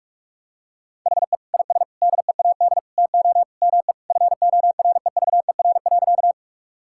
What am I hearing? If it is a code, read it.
Morse code, heard as a keyed tone, about 47 wpm, 700 Hz. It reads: HE IS DEAD TO ME FOREVER.
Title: Sentences from Top 1600 Words 45wpm